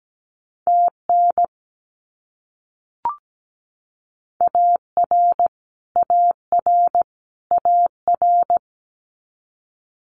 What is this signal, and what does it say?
Morse code, heard as a keyed tone, about 17 wpm, 700 Hz. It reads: TN AR AR AR
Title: US States Repeated 3x 17wpm